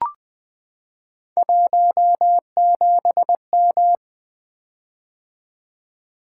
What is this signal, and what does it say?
Morse code, heard as a keyed tone, about 21 wpm, 700 Hz. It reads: 17M
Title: Three Letter-Number Combinations - 20wpm